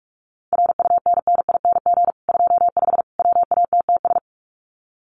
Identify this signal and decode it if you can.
Morse code, heard as a keyed tone, about 44 wpm, 700 Hz. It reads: RUNNING 25 WATTS
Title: Common QSO Phrases 45wpm